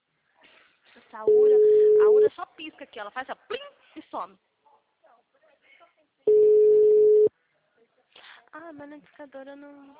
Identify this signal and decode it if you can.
Ringback tone (Europe, 425 Hz), with speech.
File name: out- -2041- - - .88680